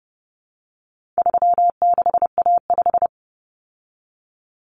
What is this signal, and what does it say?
Morse code, heard as a keyed tone, about 30 wpm, 700 Hz. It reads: 36A5
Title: Common Cut Numbers - 4-Digit 30wpm